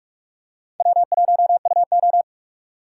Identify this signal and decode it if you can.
Morse code, heard as a keyed tone, about 41 wpm, 700 Hz. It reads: W1UO